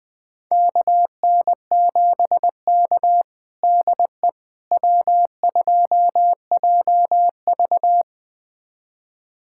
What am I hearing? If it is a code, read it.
Morse code, heard as a keyed tone, about 20 wpm, 700 Hz. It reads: KN7K DE W2JV
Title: USA Callsign DE Callsign - 4 Characters - 20wpm